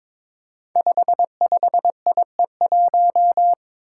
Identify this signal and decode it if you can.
Morse code, heard as a keyed tone, about 22 wpm, 700 Hz. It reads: 55IE1
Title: Five Challenging Letter-Number Combinations 22wpm